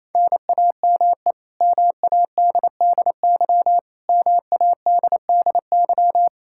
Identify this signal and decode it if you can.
Morse code, heard as a keyed tone, about 28 wpm, 700 Hz. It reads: NAME MADDY MADDY
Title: Practice QSO - Exchange 1 - Beginning - 28wpm